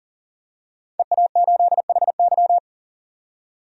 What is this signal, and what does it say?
Morse code, heard as a keyed tone, about 40 wpm, 700 Hz. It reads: EA8HY
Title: DXCC-100 Call Signs - 5 Characters - 40wpm